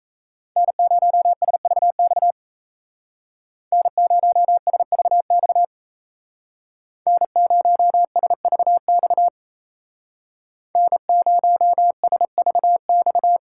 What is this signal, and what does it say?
Morse code, heard as a keyed tone, about 36 wpm, 700 Hz. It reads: N0SVX N0SVX N0SVX N0SVX